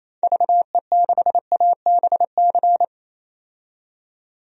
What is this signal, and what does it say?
Morse code, heard as a keyed tone, about 28 wpm, 700 Hz. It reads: VE6ABC
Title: DXCC-100 Call Signs - 6 Characters - 28wpm